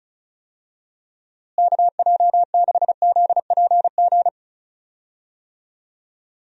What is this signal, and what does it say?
Morse code, heard as a keyed tone, about 36 wpm, 700 Hz. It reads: KJ6ZPG
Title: USA Call Signs - 6 Characters - Repeated 2x - 35wpm